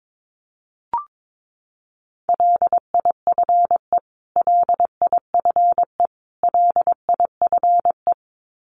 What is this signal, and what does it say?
Morse code, heard as a keyed tone, about 22 wpm, 700 Hz. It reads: LIFE LIFE LIFE
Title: Top 300 Words Repeated 22wpm